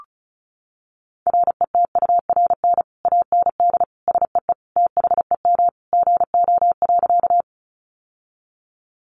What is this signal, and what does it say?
Morse code, heard as a keyed tone, about 34 wpm, 700 Hz. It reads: RETURN AND SEE THEM GO.
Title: Sentences from Top 800 Words 35wpm